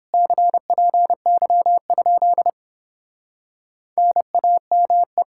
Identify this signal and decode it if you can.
Morse code, heard as a keyed tone, about 31 wpm, 700 Hz. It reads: CPY? NAME